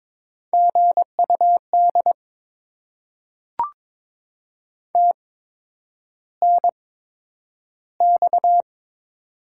Morse code, heard as a keyed tone, about 22 wpm, spaced as for 9 wpm, 700 Hz. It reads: GUD TNX
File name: Top 20 QSO Elements - 8x Character Spacing - 22wpm